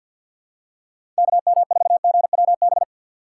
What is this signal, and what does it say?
Morse code, heard as a keyed tone, about 53 wpm, 700 Hz. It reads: KM4GWB
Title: USA Call Signs - 6 Characters - 3x Character Spacing - 50wpm